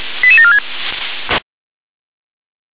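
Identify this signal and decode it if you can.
Five-tone selective call (ZVEI), heard as quick runs of five tones, tones of about 70 ms. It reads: 88746
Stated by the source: ZVEI five-tone selective call received on SDR.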